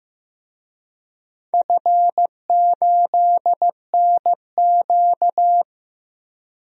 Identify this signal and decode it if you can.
Morse code, heard as a keyed tone, about 15 wpm, 700 Hz. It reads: F8NQ
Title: DXCC-100 Call Signs - 4 Characters - 15wpm